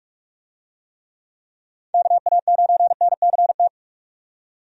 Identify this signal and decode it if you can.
Morse code, heard as a keyed tone, about 46 wpm, 700 Hz. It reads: KA9NCT
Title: USA Call Signs - 6 Characters - 45wpm